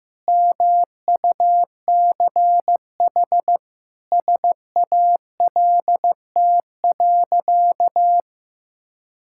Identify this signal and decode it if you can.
Morse code, heard as a keyed tone, about 15 wpm, 700 Hz. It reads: MUCH SALT.
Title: Sentences from Top 1000 Words - Review Part 5 - 15wpm